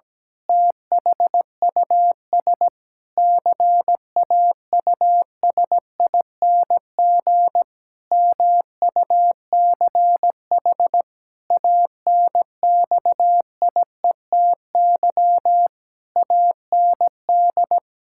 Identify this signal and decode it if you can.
Morse code, heard as a keyed tone, about 17 wpm, 700 Hz. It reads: THUS CAUSING MUCH ANXIETY AND
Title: Pangram Challenge v2 - 17wpm